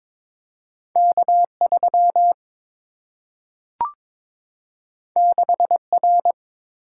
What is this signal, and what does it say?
Morse code, heard as a keyed tone, about 22 wpm, 700 Hz. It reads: K 3 6 R
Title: Single-Letter Single-Number 22wpm